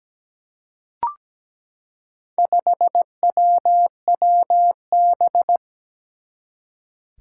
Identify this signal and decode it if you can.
Morse code, heard as a keyed tone, about 17 wpm, 700 Hz. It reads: 5WWB
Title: Four Challenging Letter-Number Combinations 17wpm